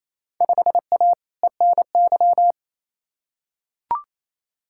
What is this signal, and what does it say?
Morse code, heard as a keyed tone, about 28 wpm, 700 Hz. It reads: 5A ENY
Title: Field Day Exchanges Encore - 28wpm